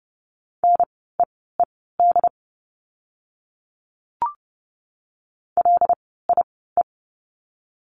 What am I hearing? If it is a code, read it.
Morse code, heard as a keyed tone, about 30 wpm, spaced as for 17 wpm, 700 Hz. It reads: NEED LIE
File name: Lesson 16f - Letter c - Words 30wpm